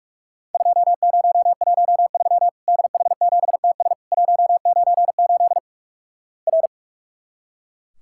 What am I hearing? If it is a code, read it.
Morse code, heard as a keyed tone, about 44 wpm, 700 Hz. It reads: 2013 DH7TS 198 R
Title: Worked All Europe DX Contest - QTC Exchange - 45wpm